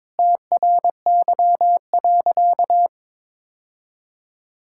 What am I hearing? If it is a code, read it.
Morse code, heard as a keyed tone, about 22 wpm, 700 Hz. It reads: TRY.
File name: practice / 22wpm / Sentences with 2-Word Phrases - 22wpm